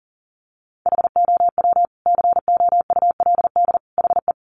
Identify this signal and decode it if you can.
Morse code, heard as a keyed tone, about 40 wpm, 700 Hz. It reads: HOW COULD HE
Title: Sentences from Top 1300 Words 40wpm